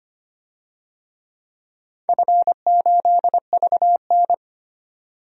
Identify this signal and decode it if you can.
Morse code, heard as a keyed tone, about 25 wpm, 700 Hz. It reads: F8VN
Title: DXCC-100 Call Signs - 4 Characters - 25wpm